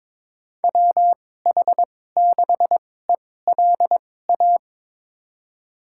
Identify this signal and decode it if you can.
Morse code, heard as a keyed tone, about 22 wpm, spaced as for 16 wpm, 700 Hz. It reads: WH6ELA
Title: USA Call Signs - 6 Characters - 2x Character Spacing - 22wpm